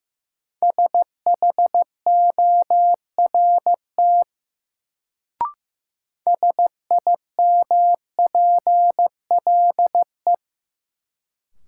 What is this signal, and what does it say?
Morse code, heard as a keyed tone, about 15 wpm, 700 Hz. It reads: SHORT SIMPLE
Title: Sentences from Top 400 Words - New Word Drill - 15wpm